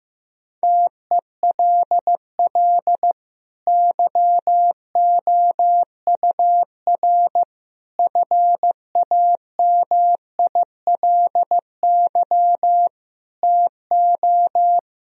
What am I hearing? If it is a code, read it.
Morse code, heard as a keyed tone, about 15 wpm, 700 Hz. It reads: TELL YOUR FAMILY TO
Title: Sentences from Top 1000 Words - Review Part 2 - 15wpm